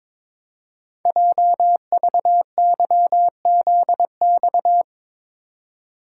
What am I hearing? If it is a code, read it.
Morse code, heard as a keyed tone, about 22 wpm, 700 Hz. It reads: JVYZX
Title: Code Groups - 5 Letters - 22wpm